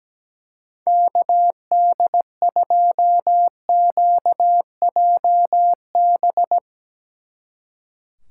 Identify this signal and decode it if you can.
Morse code, heard as a keyed tone, about 17 wpm, 700 Hz. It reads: KD2QJB
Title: Speed Racing - USA 2x3 Call Signs 17wpm